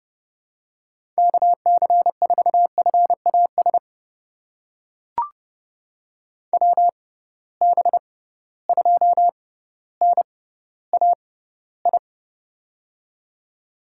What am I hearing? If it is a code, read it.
Morse code, heard as a keyed tone, about 30 wpm, spaced as for 15 wpm, 700 Hz. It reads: KC4FAS WB2NAI
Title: USA Call Signs - 6 Characters - 6x Character Spacing - 30wpm